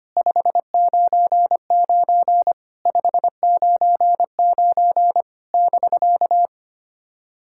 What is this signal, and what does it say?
Morse code, heard as a keyed tone, about 25 wpm, 690 Hz. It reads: 599 599 <BK>
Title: SOTA QSO Practice 25wpm